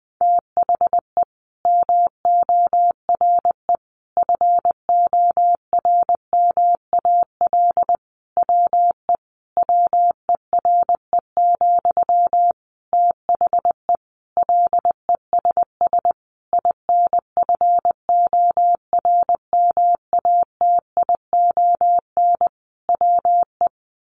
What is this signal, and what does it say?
Morse code, heard as a keyed tone, about 20 wpm, 700 Hz. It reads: THE MORE FORMAL WE WERE, THE LESS INFORMATION WE
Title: Sentences from Top 1300 Words 20wpm